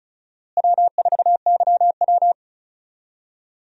Morse code, heard as a keyed tone, about 34 wpm, 700 Hz. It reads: W4YW
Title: USA Call Signs - 4 Characters - Repeated 2x - 35wpm